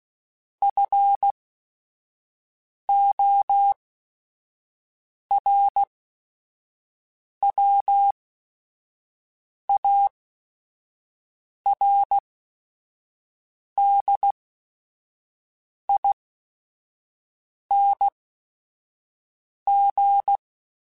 Morse code, heard as a keyed tone, about 16 wpm, spaced as for 5 wpm, 800 Hz. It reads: FORWARDING
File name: Chuckles / Chuckles-mp3s / Temptation @5